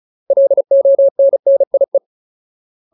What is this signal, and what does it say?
Morse code, heard as a keyed tone, about 34 wpm, 550 Hz. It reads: LONNIE
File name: Ultimate Word List / 35WPM Ultimate